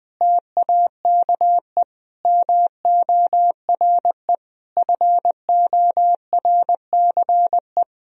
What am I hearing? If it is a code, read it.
Morse code, heard as a keyed tone, about 20 wpm, 700 Hz. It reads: TAKE MORE FORCE
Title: Sentences from Top 1000 Words - Review Part 3 - 20wpm